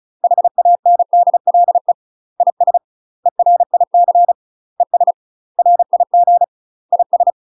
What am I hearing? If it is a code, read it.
Morse code, heard as a keyed tone, about 34 wpm, 700 Hz. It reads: HANDLE IS ERIC ES RIG IS
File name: UK MASTER QSOs TRAINING/35WPM